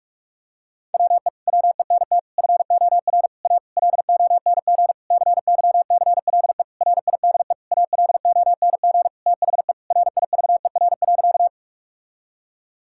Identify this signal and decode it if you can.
Morse code, heard as a keyed tone, about 41 wpm, 700 Hz. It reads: WE WENT FOR A LONG CYCLE RIDE ALONG THE RIVER.
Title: Sentences from Top 1000 Words - Review Part 6 - 45wpm